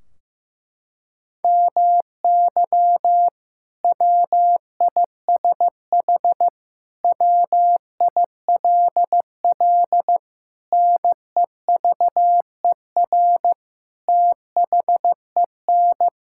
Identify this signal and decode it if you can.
Morse code, heard as a keyed tone, about 15 wpm, 700 Hz. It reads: MY WISH WILL NEVER THEN
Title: Sentences from Top 1600 Words 15wpm